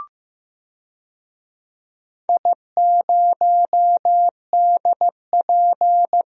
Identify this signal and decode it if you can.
Morse code, heard as a keyed tone, about 15 wpm, 700 Hz. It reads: I0DP
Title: DXCC-100 Call Signs - 4 Characters - 15wpm